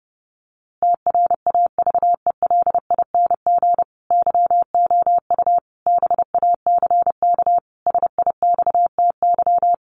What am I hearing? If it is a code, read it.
Morse code, heard as a keyed tone, about 30 wpm, 700 Hz. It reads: TRAVELING YOU BACK SIXTY
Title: A Sound Of Thunder 30wpm